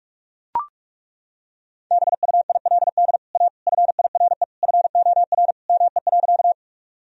Morse code, heard as a keyed tone, about 41 wpm, 700 Hz. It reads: BUILD A FIRE FOR ME.
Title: Sentences from Top 1000 Words - Review Part 1 - 45wpm